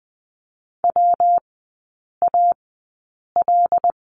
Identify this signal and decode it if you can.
Morse code, heard as a keyed tone, about 20 wpm, spaced as for 8 wpm, 700 Hz. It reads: WAL
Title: 121 Lesson 20e - Number 3 - Three Characters 20wpm